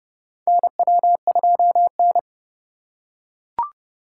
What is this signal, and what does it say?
Morse code, heard as a keyed tone, about 32 wpm, 700 Hz. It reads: NW2N
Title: USA Call Signs - 2x Character Spacing - 30wpm